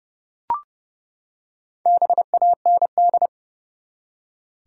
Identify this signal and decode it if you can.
Morse code, heard as a keyed tone, about 29 wpm, 700 Hz. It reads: BAND
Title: Four Letter Words - 30wpm